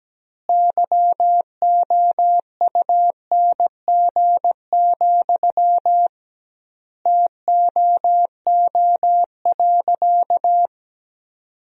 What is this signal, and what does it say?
Morse code, heard as a keyed tone, about 17 wpm, 700 Hz. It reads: YOUNG, TOO.
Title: 2x Word-Spacing - Sentences from Top 400 Words - 17wpm